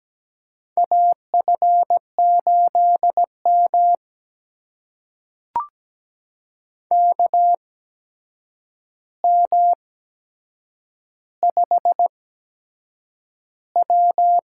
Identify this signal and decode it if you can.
Morse code, heard as a keyed tone, about 17 wpm, spaced as for 7 wpm, 700 Hz. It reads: AF8M KM5W
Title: USA Call Signs - 8x Character Spacing - 17wpm